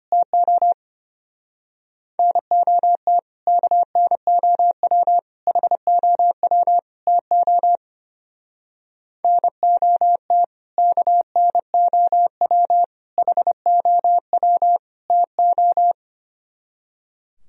Morse code, heard as a keyed tone, about 29 wpm, 700 Hz. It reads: TO NOT KNOW HOW TO NOT KNOW HOW TO
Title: Speed-Racing - Sets of 4 Words 25wpm